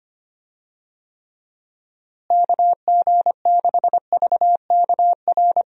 Morse code, heard as a keyed tone, about 25 wpm, 700 Hz. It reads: KG6VKR
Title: USA Call Signs - 6 Characters - Repeated 3x - 25wpm